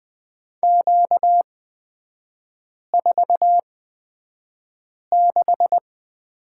Morse code, heard as a keyed tone, about 20 wpm, spaced as for 5 wpm, 690 Hz. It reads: Q46
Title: 236 Lesson 38e - Letter q - Three Characters 20wpm